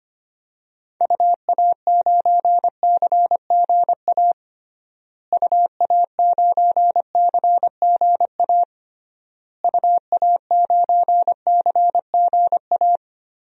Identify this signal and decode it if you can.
Morse code, heard as a keyed tone, about 25 wpm, 700 Hz. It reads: UA9CGA UA9CGA UA9CGA
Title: DXCC-100 Call Signs - 6 Characters - Repeated 3x - 25wpm